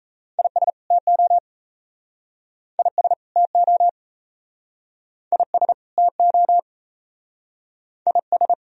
Morse code, heard as a keyed tone, about 38 wpm, 700 Hz. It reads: IS TO IS TO IS TO IS